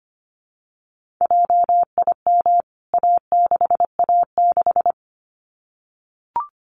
Morse code, heard as a keyed tone, about 25 wpm, 700 Hz. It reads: JIM A6A6
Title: CWT Exchange 25wpm